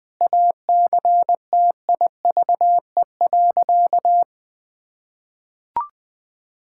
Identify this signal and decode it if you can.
Morse code, heard as a keyed tone, about 20 wpm, 700 Hz. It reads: ACTIVE.
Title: Sentences from Top 1000 Words - Review Part 6 - 20wpm